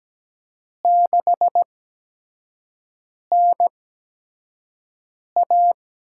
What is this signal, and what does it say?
Morse code, heard as a keyed tone, about 17 wpm, spaced as for 5 wpm, 700 Hz. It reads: 6NA
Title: USA Call Signs - 4 Characters - 8x Character Spacing - 17wpm